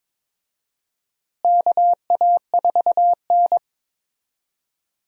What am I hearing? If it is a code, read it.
Morse code, heard as a keyed tone, about 22 wpm, 700 Hz. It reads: KA4N
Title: USA Call Signs - 4 Characters - Repeated 2x - 22wpm